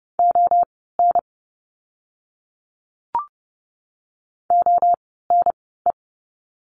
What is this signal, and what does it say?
Morse code, heard as a keyed tone, about 30 wpm, spaced as for 17 wpm, 700 Hz. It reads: ON ONE
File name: course / 30:10wpm / 237 Lesson 38f - Letter q - Words 30wpm